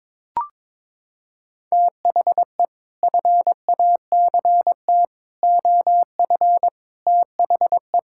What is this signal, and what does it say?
Morse code, heard as a keyed tone, about 22 wpm, 700 Hz. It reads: THE FACT OF THE
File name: practice / 22wpm / Sets of 6 Words - 22wpm